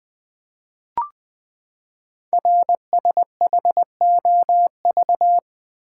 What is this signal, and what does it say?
Morse code, heard as a keyed tone, about 20 wpm, 700 Hz. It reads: RSHOV
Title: Code Groups - 5 Letter:Numbers - 20wpm